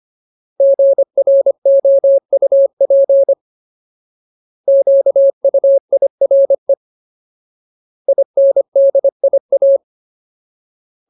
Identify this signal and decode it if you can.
Morse code, heard as a keyed tone, about 25 wpm, 550 Hz. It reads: GROUP QUIRE INDIA
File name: Ultimate Word List / 25WPM Ultimate